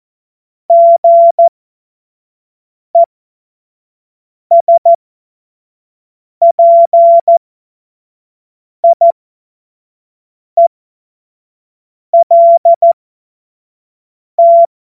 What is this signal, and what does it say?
Morse code, heard as a keyed tone, about 14 wpm, spaced as for 5 wpm, 680 Hz. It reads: GESPIELT
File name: Morsekurs / Witze / Witz 02. Februar 2021, 25 F70